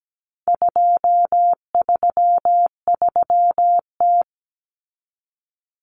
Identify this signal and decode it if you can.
Morse code, heard as a keyed tone, about 17 wpm, 700 Hz. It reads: 233T
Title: Common Cut Numbers - 4-Digit 17wpm